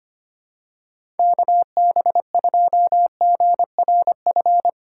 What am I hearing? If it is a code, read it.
Morse code, heard as a keyed tone, about 25 wpm, 700 Hz. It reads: KB2GRF
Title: USA Call Signs - 3x Character Spacing - 25wpm